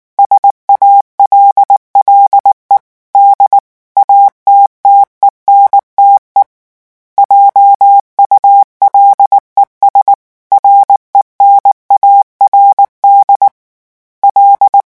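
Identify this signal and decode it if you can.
Morse code, heard as a keyed tone, about 19 wpm, 800 Hz. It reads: SALLE D ATTENTE JULES RENARD L
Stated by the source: French Morse revision 8 drill of random five-letter groups at 18 words per minute.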